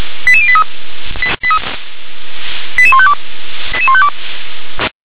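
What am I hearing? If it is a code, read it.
Five-tone selective call (ZVEI), heard as quick runs of five tones, tones of about 70 ms. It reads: 88083 88152
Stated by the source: ZVEI five-tone selective call received by SDR.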